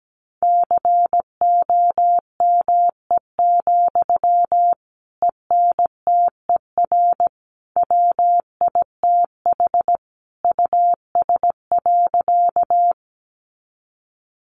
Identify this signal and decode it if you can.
Morse code, heard as a keyed tone, about 17 wpm, 700 Hz. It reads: COME, ENTER WITH US.
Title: Sentences from Top 1300 Words 17wpm